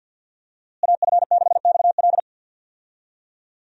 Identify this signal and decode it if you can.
Morse code, heard as a keyed tone, about 52 wpm, 700 Hz. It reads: AF6XL
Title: USA Call Signs - 5 Characters - Repeated 3x - 50wpm